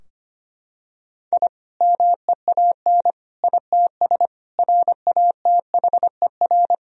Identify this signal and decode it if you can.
Morse code, heard as a keyed tone, about 25 wpm, 700 Hz. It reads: I MEAN ITS RATHER
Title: Sentences from Top 1000 Words - Review Part 4 - 25wpm